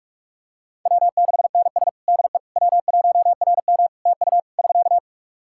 Morse code, heard as a keyed tone, about 46 wpm, 700 Hz. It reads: W6NS DE W1RM TU <SK>
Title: K1USN SST QSO Practice 45wpm